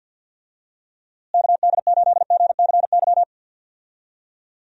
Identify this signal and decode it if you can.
Morse code, heard as a keyed tone, about 47 wpm, 700 Hz. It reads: KD8GCY
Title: USA Call Signs - 6 Characters - Repeated 3x - 50wpm